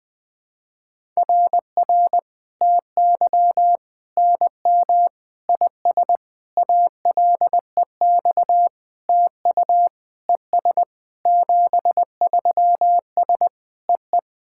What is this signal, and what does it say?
Morse code, heard as a keyed tone, about 20 wpm, 700 Hz. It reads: RR TY NM IS ALEX TU ES 73S EE
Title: SOTA QSO Practice 20wpm